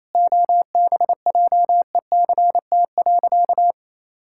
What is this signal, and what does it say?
Morse code, heard as a keyed tone, about 28 wpm, 700 Hz. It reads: OBJECT.
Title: Sentences from Top 1000 Words - Review Part 5 - 28wpm